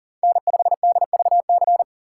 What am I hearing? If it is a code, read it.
Morse code, heard as a keyed tone, about 40 wpm, 700 Hz. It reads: N5DVC
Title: USA Call Signs - 5 Characters - 3x Character Spacing - 40wpm